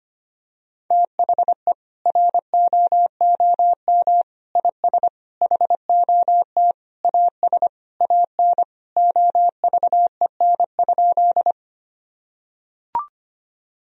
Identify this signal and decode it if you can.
Morse code, heard as a keyed tone, about 25 wpm, 700 Hz. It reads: THE ROOM IS HOT AS AN OVEN?
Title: Sentences from Top 1000 Words - Review Part 4 - 25wpm